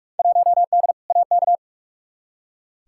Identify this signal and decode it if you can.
Morse code, heard as a keyed tone, about 43 wpm, 700 Hz. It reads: 1D AK
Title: Field Day Exchanges - 45wpm